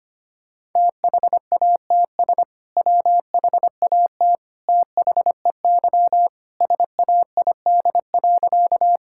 Morse code, heard as a keyed tone, about 25 wpm, 700 Hz. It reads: THATS WHAT THEY SAID.
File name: practice / 25wpm / Sentences with 2-Word Phrases - 25wpm